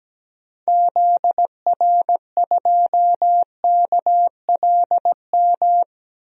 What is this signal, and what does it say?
Morse code, heard as a keyed tone, about 17 wpm, 700 Hz. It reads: ZR2KLM
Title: DXCC-100 Call Signs - 6 Characters - Repeated 2x - 17wpm